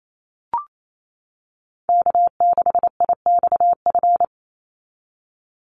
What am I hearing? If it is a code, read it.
Morse code, heard as a keyed tone, about 28 wpm, 700 Hz. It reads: K6IXF
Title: USA Call Signs 28wpm